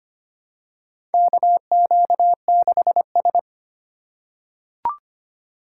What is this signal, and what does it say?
Morse code, heard as a keyed tone, about 25 wpm, 700 Hz. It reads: KQ6S
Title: USA Call Signs - 4 Characters - 4x Character Spacing - 25wpm